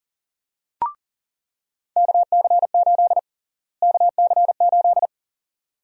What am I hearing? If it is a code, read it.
Morse code, heard as a keyed tone, about 40 wpm, 700 Hz. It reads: KC8 KC8
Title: USA Call Sign Prefixes - Repeated 2x - 40wpm